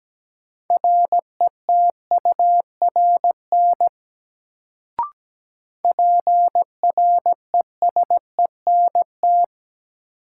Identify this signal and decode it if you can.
Morse code, heard as a keyed tone, about 17 wpm, 700 Hz. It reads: RETURN PRESENT
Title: Sentences from Top 800 Words - New Word Drill - 17wpm